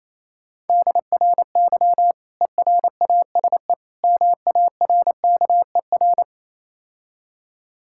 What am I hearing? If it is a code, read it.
Morse code, heard as a keyed tone, about 28 wpm, 700 Hz. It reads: DRY ERASE MARKER